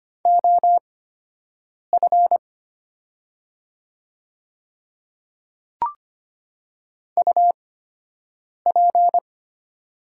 Morse code, heard as a keyed tone, about 25 wpm, spaced as for 7 wpm, 700 Hz. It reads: OF UP
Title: 8x Character Spacing - Top 100 Words - 25wpm